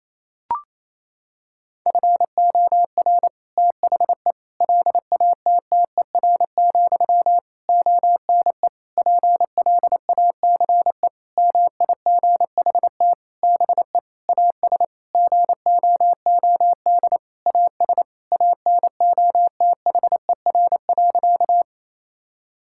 Morse code, heard as a keyed tone, about 27 wpm, 700 Hz. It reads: FOR THE LATTER, ONE PLACE MIGHT BE AS GOOD AS ANOTHER.
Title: Sentences from Top 1600 Words 28wpm